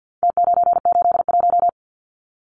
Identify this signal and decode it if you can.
Morse code, heard as a keyed tone, about 50 wpm, 700 Hz. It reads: T981
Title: Numbers - Contest Style - 4-Digit 50wpm